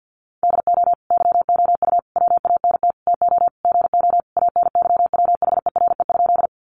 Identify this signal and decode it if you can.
Morse code, heard as a keyed tone, about 48 wpm, 700 Hz. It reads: DO YOU WANT TO GO ANYWHERE?
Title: Sentences from Top 1400 Words 50wpm